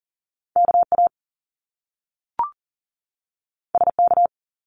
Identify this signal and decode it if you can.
Morse code, heard as a keyed tone, about 38 wpm, 700 Hz. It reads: KA SK
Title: Top Two Letter Combinations in Words 40wpm